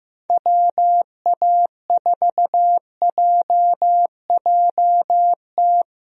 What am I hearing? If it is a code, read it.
Morse code, heard as a keyed tone, about 16 wpm, 700 Hz. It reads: WA4JJT
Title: USA Call Signs - 6 Characters - Repeated 2x - 15wpm